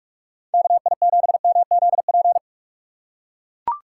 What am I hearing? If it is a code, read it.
Morse code, heard as a keyed tone, about 42 wpm, 690 Hz. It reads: KI7MZP
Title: USA Call Signs - 8x Character Spacing - 45wpm